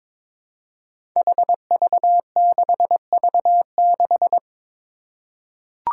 Morse code, heard as a keyed tone, about 22 wpm, 700 Hz. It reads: HV6V6
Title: Five Challenging Letter-Number Combinations 22wpm